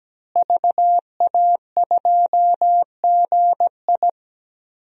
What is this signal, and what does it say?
Morse code, heard as a keyed tone, about 17 wpm, 700 Hz. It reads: VA2GI
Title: DXCC-100 Call Signs - 5 Characters - Repeated 2x - 17wpm